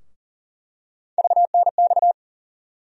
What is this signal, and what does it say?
Morse code, heard as a keyed tone, about 40 wpm, 700 Hz. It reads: VNX
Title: USA Call Sign Postfixes - 3 Letters 40wpm